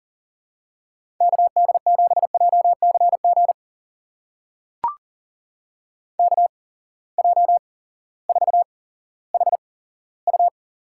Morse code, heard as a keyed tone, about 40 wpm, spaced as for 18 wpm, 700 Hz. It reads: KD7JCG KJ4HU
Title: USA Call Signs - 6 Characters - 8x Character Spacing - 40wpm